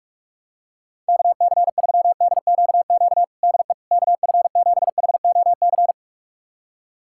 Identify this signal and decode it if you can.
Morse code, heard as a keyed tone, about 41 wpm, 700 Hz. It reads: KC3DQQ DE KF7HOC
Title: USA Callsign DE Callsign 45wpm